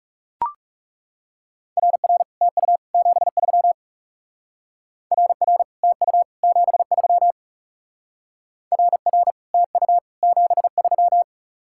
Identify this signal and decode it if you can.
Morse code, heard as a keyed tone, about 37 wpm, 700 Hz. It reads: RR TU 73 RR TU 73 RR TU 73
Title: Speed-Racing Common QSO Phrases Encore - 30wpm